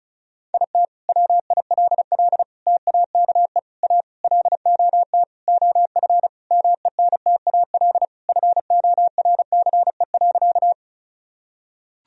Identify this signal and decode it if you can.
Morse code, heard as a keyed tone, about 36 wpm, 700 Hz. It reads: IT WILL TAKE A LOT OF MENTAL FORCE.
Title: Sentences from Top 1000 Words - Review Part 4 - 35wpm